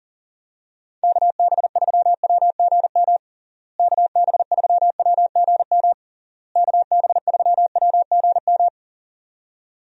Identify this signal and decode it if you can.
Morse code, heard as a keyed tone, about 37 wpm, 700 Hz. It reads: KB3WGM KB3WGM KB3WGM
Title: USA Call Signs - 6 Characters - Repeated 3x - 40wpm